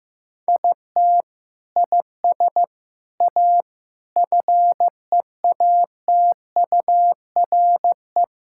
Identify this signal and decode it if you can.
Morse code, heard as a keyed tone, about 16 wpm, 700 Hz. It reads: IT IS A FEATURE
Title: Sentences from Top 1000 Words - Review Part 5 - 15wpm